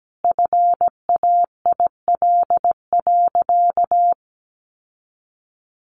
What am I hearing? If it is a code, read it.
Morse code, heard as a keyed tone, about 17 wpm, 700 Hz. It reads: FAIL.
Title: Sentences from Top 1100 Words 17wpm